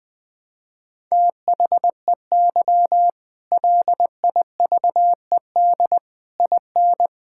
Morse code, heard as a keyed tone, about 20 wpm, 700 Hz. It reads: THEY LIVED IN
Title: Sentences with 8-Word Phrases - 20wpm